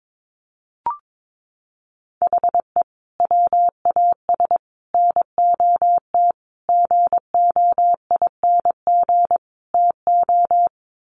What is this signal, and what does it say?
Morse code, heard as a keyed tone, about 22 wpm, 700 Hz. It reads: HE WAS NOT GOING TO
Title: Sets of 5 Words 22wpm